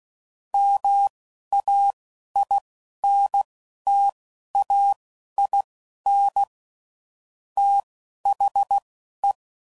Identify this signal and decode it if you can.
Morse code, heard as a keyed tone, about 16 wpm, spaced as for 11 wpm, 800 Hz. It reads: MAINTAIN THE
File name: CW Code Groups @10WPM / Technical Subjects / XC